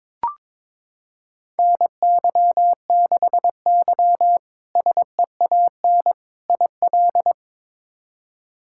Morse code, heard as a keyed tone, about 22 wpm, 700 Hz. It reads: NY6Y SEAN IL
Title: K1USN SST QSO Practice - 22wpm